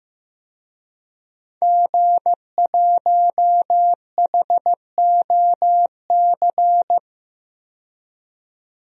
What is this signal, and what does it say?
Morse code, heard as a keyed tone, about 15 wpm, 700 Hz. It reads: G1HOC
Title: DXCC-100 Call Signs - 5 Characters - 15wpm